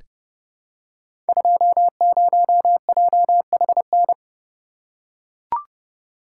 Morse code, heard as a keyed tone, about 28 wpm, 700 Hz. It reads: 20JHN
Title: Code Groups - 5 Letter:Numbers - 30wpm